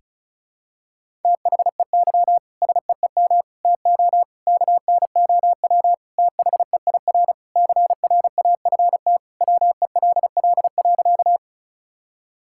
Morse code, heard as a keyed tone, about 34 wpm, 700 Hz. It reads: THEY SEEM TO KNOW THEIR CRAFT WELL.
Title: Sentences from Top 1000 Words - Review Part 6 - 35wpm